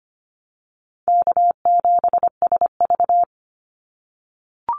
Morse code, heard as a keyed tone, about 25 wpm, 700 Hz. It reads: K7SV
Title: Easy Call Signs - 25wpm